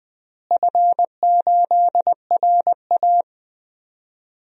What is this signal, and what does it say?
Morse code, heard as a keyed tone, about 20 wpm, 700 Hz. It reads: F8RA
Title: DXCC-100 Call Signs - 4 Characters - Repeated 3x - 20wpm